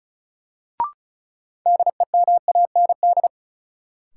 Morse code, heard as a keyed tone, about 34 wpm, 700 Hz. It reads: DEMAND
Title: Sentences from Top 800 Words - New Word Drill - 35wpm